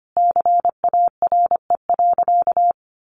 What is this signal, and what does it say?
Morse code, heard as a keyed tone, about 25 wpm, 700 Hz. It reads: CARE.
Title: Sentences from Top 300 Words 25wpm